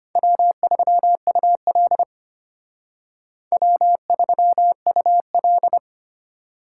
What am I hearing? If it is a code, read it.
Morse code, heard as a keyed tone, about 32 wpm, 700 Hz. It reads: W3UL W3UL